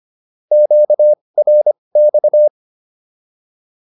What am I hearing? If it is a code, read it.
Morse code, heard as a keyed tone, about 25 wpm, spaced as for 20 wpm, 600 Hz. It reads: QRX